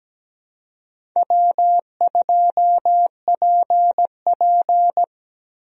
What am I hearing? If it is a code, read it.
Morse code, heard as a keyed tone, about 17 wpm, 700 Hz. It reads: W2PP
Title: USA Call Signs - 4 Characters - Repeated 3x - 17wpm